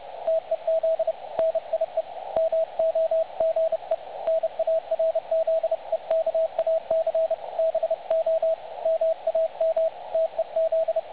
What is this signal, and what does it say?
Morse code, heard as a keyed tone, about 30 wpm, 660 Hz, with radio noise. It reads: TEZ NIE MOGE NARZEKAC BO MAM TEZ
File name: dlazaawansowanych / nagraniazpasma / websdr recording 3565.0kHz